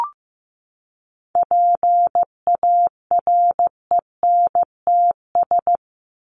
Morse code, heard as a keyed tone, about 15 wpm, 700 Hz. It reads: PARENTS
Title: ES:S Suffix - Top 100 Words with Base Words 15wpm